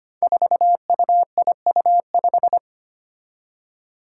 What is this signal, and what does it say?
Morse code, heard as a keyed tone, about 25 wpm, 700 Hz. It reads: 4UIU5
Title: Five Challenging Letter-Number Combinations 25wpm